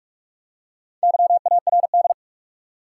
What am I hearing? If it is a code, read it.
Morse code, heard as a keyed tone, about 46 wpm, 700 Hz. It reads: YARD
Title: Four Letter Words - 45wpm